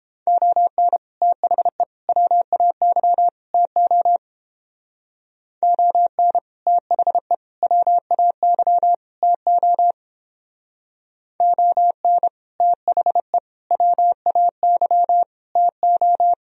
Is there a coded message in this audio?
Yes: Morse code from 0.3 to 16.3 s, heard as a keyed tone, about 29 wpm, 700 Hz. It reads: ON THE WAY TO ON THE WAY TO ON THE WAY TO